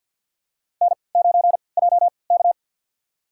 Morse code, heard as a keyed tone, about 52 wpm, spaced as for 29 wpm, 700 Hz. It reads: N9JK
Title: USA Call Signs - 4 Characters - 3x Character Spacing - 50wpm